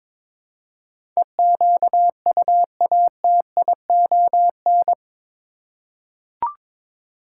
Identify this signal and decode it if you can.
Morse code, heard as a keyed tone, about 21 wpm, 700 Hz. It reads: EQUATION
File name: practice / 22wpm / Eight Letter Words - 22wpm